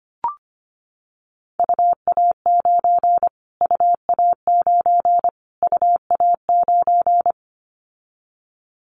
Morse code, heard as a keyed tone, about 25 wpm, 700 Hz. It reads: UA9 UA9 UA9
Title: Callsign Prefixes for Top 25 DXCC - Repeated 3x - 25wpm